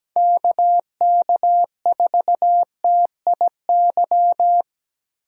Morse code, heard as a keyed tone, about 17 wpm, 700 Hz. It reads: KK4TIY
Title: USA Call Signs - 8x Character Spacing - 17wpm